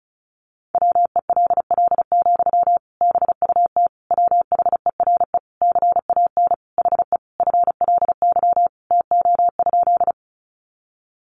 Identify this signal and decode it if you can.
Morse code, heard as a keyed tone, about 34 wpm, 700 Hz. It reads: WELL, BUT WHERE CAN HE FLY TO?
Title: Sentences from Top 500 Words 35wpm